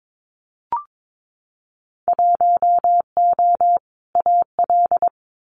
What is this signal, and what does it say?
Morse code, heard as a keyed tone, about 22 wpm, 700 Hz. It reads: 1O AL
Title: Winter Field Day Exchanges Encore 22wpm